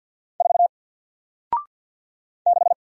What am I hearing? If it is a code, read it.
Morse code, heard as a keyed tone, about 48 wpm, 700 Hz. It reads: 4 6
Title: Single Character (Letter:Number:Punctuation:Prosign) Alt 50wpm